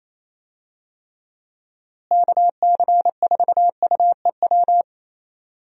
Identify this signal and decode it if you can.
Morse code, heard as a keyed tone, about 28 wpm, 700 Hz. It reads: KC4UEW
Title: USA Call Signs - 6 Characters - Repeated 2x - 28wpm